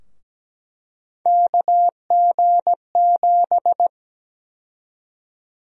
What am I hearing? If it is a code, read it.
Morse code, heard as a keyed tone, about 17 wpm, 700 Hz. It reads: KG7
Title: USA Call Sign Prefixes - 17wpm